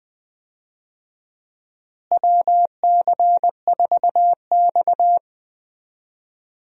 Morse code, heard as a keyed tone, about 20 wpm, 700 Hz. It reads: WC4X
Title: USA Call Signs - 4 Characters - Repeated 2x - 20wpm